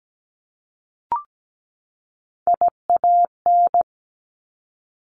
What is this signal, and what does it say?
Morse code, heard as a keyed tone, about 17 wpm, 700 Hz. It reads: IAN
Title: Top Three Letter Combinations in Words 17wpm